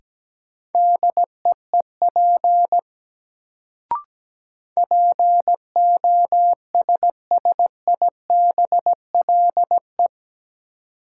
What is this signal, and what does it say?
Morse code, heard as a keyed tone, about 17 wpm, 700 Hz. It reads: DEEP POSSIBLE
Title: Sentences from Top 500 Words - New Word Drill - 17wpm